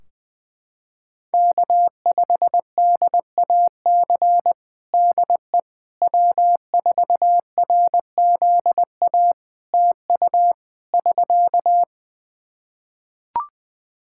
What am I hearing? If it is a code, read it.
Morse code, heard as a keyed tone, about 20 wpm, 700 Hz. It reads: K5DAC DE W4RZA TU <SK>
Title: Practice QSO - Exchange 4 - Ending - 20wpm